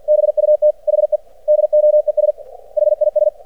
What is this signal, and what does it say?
Morse code, heard as a keyed tone, about 49 wpm, 600 Hz. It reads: BUT HE DOES HIS
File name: archived / CRSnet